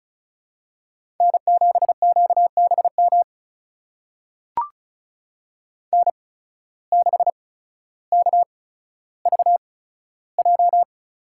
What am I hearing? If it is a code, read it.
Morse code, heard as a keyed tone, about 36 wpm, spaced as for 15 wpm, 700 Hz. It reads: N7QBM N6KVJ
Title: USA Call Signs - 5 Characters - 8x Character Spacing - 35wpm